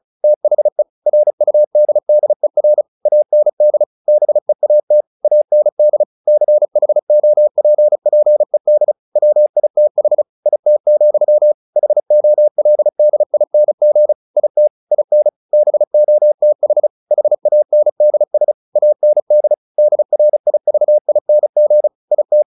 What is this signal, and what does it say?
Morse code, heard as a keyed tone, about 34 wpm, 600 Hz. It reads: THE RUDDER AND BEAT AND CHOPPED WITH IT, HOLDING IT IN BOTH HANDS AND DRIVING IT
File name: the old man and the sea 2 35wpm